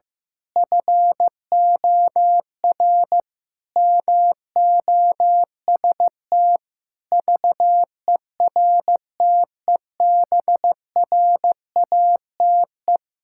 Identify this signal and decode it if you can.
Morse code, heard as a keyed tone, about 15 wpm, 700 Hz. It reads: FOR MOST VERTEBRATE
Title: Pangram Challenge v2 - 15wpm